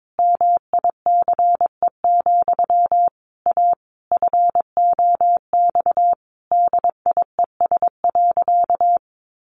Morse code, heard as a keyed tone, about 22 wpm, 700 Hz. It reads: MICE, A FOX DIES.
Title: A Sound Of Thunder 22wpm